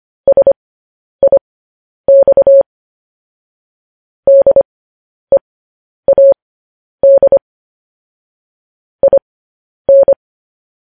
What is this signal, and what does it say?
Morse code, heard as a keyed tone, about 25 wpm, spaced as for 10 wpm, 550 Hz. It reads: SIX DEAD IN
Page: news headlines in morse code at 10 wpm